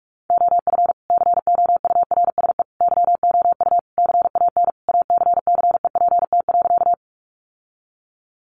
Morse code, heard as a keyed tone, about 44 wpm, 700 Hz. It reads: OF COURSE YOU CAN ACCEPT.
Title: Sentences from Top 1300 Words 45wpm